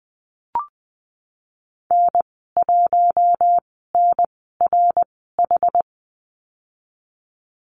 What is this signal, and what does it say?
Morse code, heard as a keyed tone, about 20 wpm, spaced as for 15 wpm, 700 Hz. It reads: N1NRH
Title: 088 Lesson 15g - Number 5 - Call Signs 20wpm